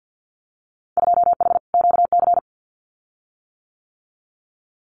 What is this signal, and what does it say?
Morse code, heard as a keyed tone, about 50 wpm, 700 Hz. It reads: 2H QC
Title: Winter Field Day Exchanges Encore 50wpm